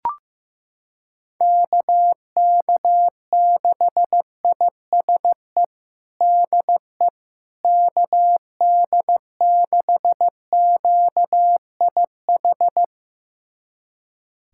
Morse code, heard as a keyed tone, about 16 wpm, 700 Hz. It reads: KK6ISE DE KD6QIH
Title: USA Callsign DE Callsign - 6 Characters - 15wpm